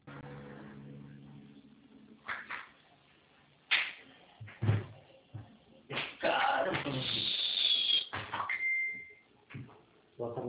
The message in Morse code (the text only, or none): none